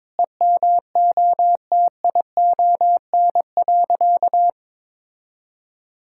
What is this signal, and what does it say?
Morse code, heard as a keyed tone, about 22 wpm, 700 Hz. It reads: EMOTION.
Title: Sentences from Top 1600 Words 22wpm